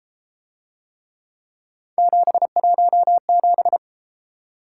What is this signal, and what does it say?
Morse code, heard as a keyed tone, about 31 wpm, 700 Hz. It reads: 717